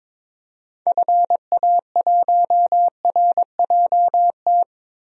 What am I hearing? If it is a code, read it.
Morse code, heard as a keyed tone, about 21 wpm, 700 Hz. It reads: FA1RJT
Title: DXCC-100 Call Signs - 6 Characters - 22wpm